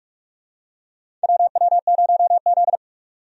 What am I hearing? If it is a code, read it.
Morse code, heard as a keyed tone, about 44 wpm, 700 Hz. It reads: WW0Z
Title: USA Call Signs - 4 Characters - 45wpm